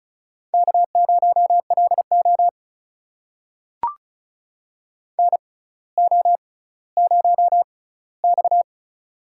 Morse code, heard as a keyed tone, about 36 wpm, spaced as for 18 wpm, 700 Hz. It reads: K0LO NO0X
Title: USA Call Signs - 6x Character Spacing - 35wpm